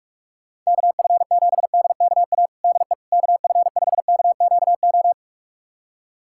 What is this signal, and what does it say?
Morse code, heard as a keyed tone, about 41 wpm, 700 Hz. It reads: KF7DKA DE KF5KQO
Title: USA Callsign DE Callsign - 6 Characters - 45wpm